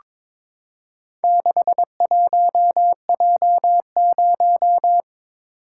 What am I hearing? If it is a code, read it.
Morse code, heard as a keyed tone, about 22 wpm, 700 Hz. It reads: 61J0
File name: practice / 22wpm / Four Letter-Number Combinations - 22wpm